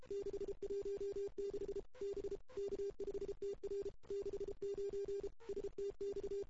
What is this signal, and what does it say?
Morse code, heard as a keyed tone, about 33 wpm, 400 Hz. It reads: 616 B K5TR 69 STX
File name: CW / 2008-ss-cw-k5tr-op-n5rz -03:00